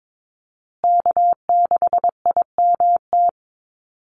Morse code, heard as a keyed tone, about 22 wpm, 700 Hz. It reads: K6IMT
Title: USA Call Signs Repeated 3x 22wpm